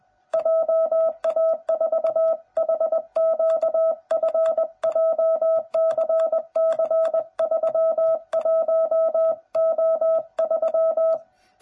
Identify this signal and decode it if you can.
Morse code, heard as a keyed tone, about 20 wpm, 660 Hz. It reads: JA4HQF JCC31O3
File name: jh8cbh / 2021cw / cw09b